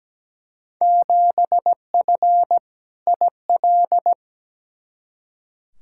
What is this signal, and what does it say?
Morse code, heard as a keyed tone, about 17 wpm, 700 Hz. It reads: 7F IL
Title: Field Day Exchanges Encore - 17wpm